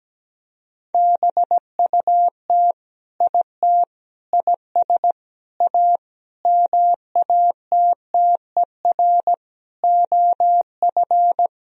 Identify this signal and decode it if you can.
Morse code, heard as a keyed tone, about 17 wpm, 700 Hz. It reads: BUT IT IS A MATTER OF
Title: Sentences from Top 1000 Words - Review Part 3 - 17wpm